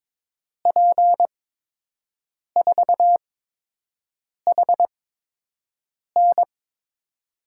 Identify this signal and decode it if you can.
Morse code, heard as a keyed tone, about 22 wpm, spaced as for 6 wpm, 700 Hz. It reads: P4HN
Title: USA Call Signs - 5 Characters - 8x Character Spacing - 22wpm